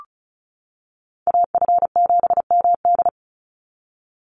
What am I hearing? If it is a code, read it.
Morse code, heard as a keyed tone, about 34 wpm, 700 Hz. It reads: AF7MD
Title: NA SOTA Chaser Callsigns 35wpm